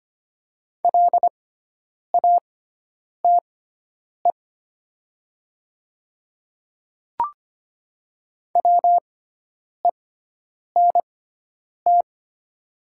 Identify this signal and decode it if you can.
Morse code, heard as a keyed tone, about 25 wpm, spaced as for 9 wpm, 700 Hz. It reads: LATE WENT
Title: 6x Character Spacing - Top 500 Words - 25wpm